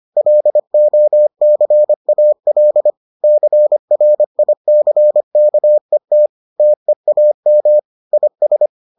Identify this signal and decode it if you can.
Morse code, heard as a keyed tone, about 25 wpm, 600 Hz. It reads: LOCAL CRICKET TEAM IS